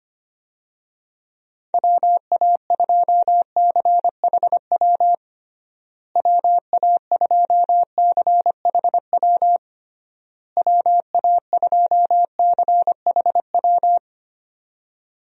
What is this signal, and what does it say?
Morse code, heard as a keyed tone, about 25 wpm, 700 Hz. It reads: WA2CHW WA2CHW WA2CHW
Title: USA Call Signs - 6 Characters - Repeated 3x - 25wpm